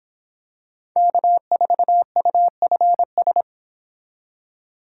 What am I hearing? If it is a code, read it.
Morse code, heard as a keyed tone, about 25 wpm, 700 Hz. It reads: K4UFS